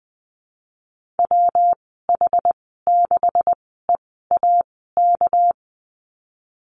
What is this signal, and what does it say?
Morse code, heard as a keyed tone, about 20 wpm, 700 Hz. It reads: W H 6 E A K
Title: 238 Lesson 38g - Letter q - Call Signs 20wpm